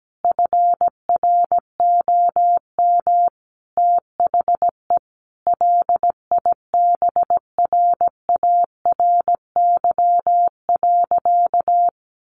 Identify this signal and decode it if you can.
Morse code, heard as a keyed tone, about 17 wpm, 700 Hz. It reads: FROM THE LIBRARY.
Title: Sentences from Top 1300 Words 17wpm